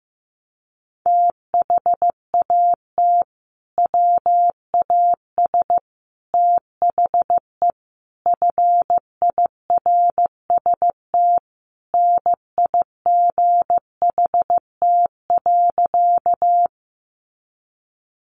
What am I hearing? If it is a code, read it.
Morse code, heard as a keyed tone, about 15 wpm, 700 Hz. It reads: THAT WAS THE FIRST NIGHT.
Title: Sentences from Top 300 Words 15wpm